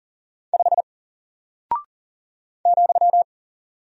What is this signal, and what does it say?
Morse code, heard as a keyed tone, about 40 wpm, 700 Hz. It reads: 5 ,
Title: Single Character (Letter:Number:Punctuation:Prosign) Alt 40wpm